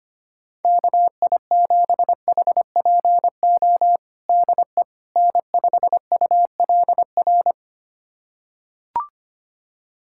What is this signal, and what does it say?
Morse code, heard as a keyed tone, about 25 wpm, 700 Hz. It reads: KI7HPO DE N5ULR
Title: USA Callsign DE Callsign 25wpm